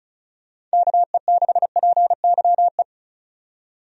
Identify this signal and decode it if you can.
Morse code, heard as a keyed tone, about 34 wpm, 700 Hz. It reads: KE6PYE
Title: USA Call Signs - 6 Characters - 35wpm